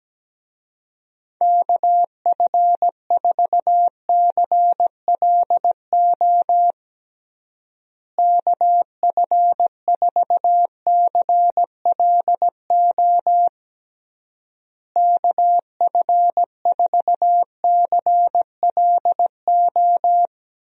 Morse code, heard as a keyed tone, about 17 wpm, 700 Hz. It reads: KF4CLO KF4CLO KF4CLO
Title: USA Call Signs - 6 Characters - Repeated 3x - 17wpm